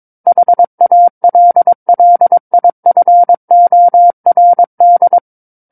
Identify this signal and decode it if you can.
Morse code, heard as a keyed tone, about 22 wpm, 700 Hz. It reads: HALLIFORD
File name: Book One / 13-Chapter 13 - 22 WPM